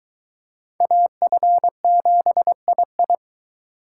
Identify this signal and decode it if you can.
Morse code, heard as a keyed tone, about 23 wpm, 700 Hz. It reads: AF7II